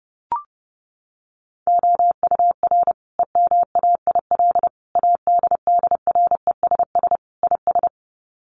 Morse code, heard as a keyed tone, about 30 wpm, 700 Hz. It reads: OUR EMAIL ADDRESS IS
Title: Sets of 5 Words Encore2 30wpm